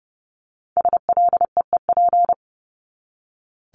Morse code, heard as a keyed tone, about 30 wpm, 700 Hz. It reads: SLEEP
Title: Top 500 Words 30wpm